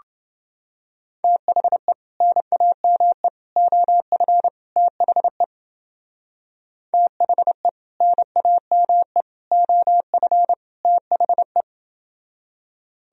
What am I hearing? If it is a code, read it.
Morse code, heard as a keyed tone, about 29 wpm, 700 Hz. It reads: THE NAME OF THE THE NAME OF THE